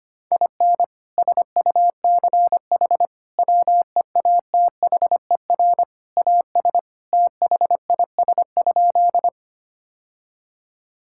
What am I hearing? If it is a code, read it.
Morse code, heard as a keyed tone, about 25 wpm, 700 Hz. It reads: IN SUCH WEATHER AS THIS?
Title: Sentences from Top 1000 Words - Review Part 5 - 25wpm